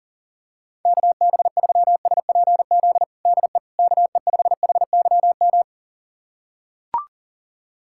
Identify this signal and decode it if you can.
Morse code, heard as a keyed tone, about 40 wpm, 700 Hz. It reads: KB3SPZ DE KE5HYM
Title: USA Callsign DE Callsign - 6 Characters - 40wpm